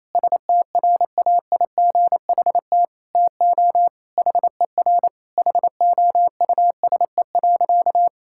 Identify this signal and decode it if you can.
Morse code, heard as a keyed tone, about 28 wpm, 700 Hz. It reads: STRAIGHT TO HER HOUSE.
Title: Sentences from Top 1000 Words - Review Part 6 - 28wpm